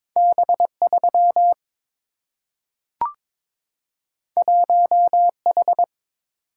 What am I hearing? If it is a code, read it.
Morse code, heard as a keyed tone, about 22 wpm, 690 Hz. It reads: B3 1H
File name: practice / 22wpm / Two Letter-Number Combinations - 22wpm